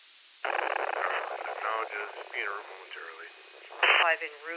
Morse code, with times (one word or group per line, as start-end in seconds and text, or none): none